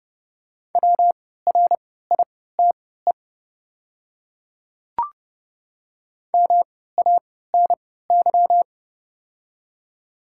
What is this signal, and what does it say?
Morse code, heard as a keyed tone, about 30 wpm, spaced as for 17 wpm, 690 Hz. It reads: WRITE MANY
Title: 3x Character Spacing - Top 100 Words - 30wpm